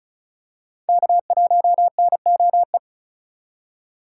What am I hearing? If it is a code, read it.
Morse code, heard as a keyed tone, about 34 wpm, 700 Hz. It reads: K1NOE
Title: USA Call Signs - 5 Characters - 2x Character Spacing - 35wpm